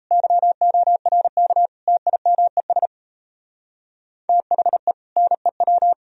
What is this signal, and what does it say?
Morse code, heard as a keyed tone, about 32 wpm, 700 Hz. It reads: YORK TIMES THE NEW
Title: Speed-Racing - Sets of 4 Words 28wpm